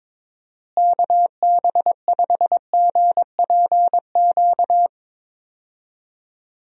Morse code, heard as a keyed tone, about 22 wpm, 700 Hz. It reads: KB5GPQ
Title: USA Call Signs - 6 Characters - Repeated 3x - 22wpm